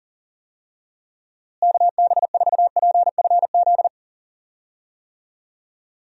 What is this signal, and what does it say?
Morse code, heard as a keyed tone, about 40 wpm, 700 Hz. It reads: KB4PFZ
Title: USA Call Signs - 6 Characters - Repeated 2x - 40wpm